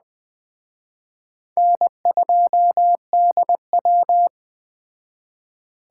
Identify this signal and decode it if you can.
Morse code, heard as a keyed tone, about 20 wpm, 700 Hz. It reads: N2DW
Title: Speed Racing - USA 1x2 Call Signs 17wpm